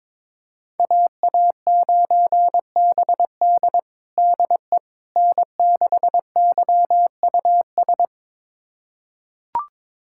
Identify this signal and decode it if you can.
Morse code, heard as a keyed tone, about 22 wpm, 700 Hz. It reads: AA9BD DE N6YUS
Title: USA Callsign DE Callsign 22wpm